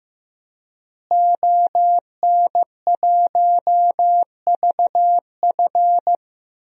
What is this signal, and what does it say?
Morse code, heard as a keyed tone, about 15 wpm, 700 Hz. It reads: ON1VF
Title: DXCC-100 Call Signs - 5 Characters - Repeated 3x - 15wpm